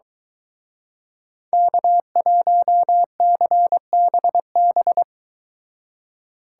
Morse code, heard as a keyed tone, about 23 wpm, 700 Hz. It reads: K1CBB